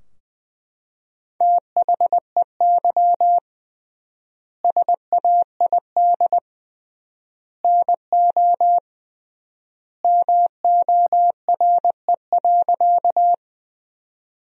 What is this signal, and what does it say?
Morse code, heard as a keyed tone, about 21 wpm, 700 Hz. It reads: THEY SAID NO MORE.
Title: 3x Word-Spacing - Sentences from Top 100 Words - 20wpm